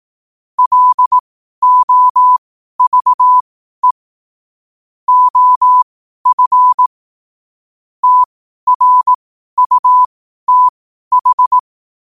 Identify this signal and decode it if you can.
Morse code, heard as a keyed tone, about 18 wpm, spaced as for 12 wpm, 1000 Hz. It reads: LOVE OF TRUTH
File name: qotdfiles / 12wpm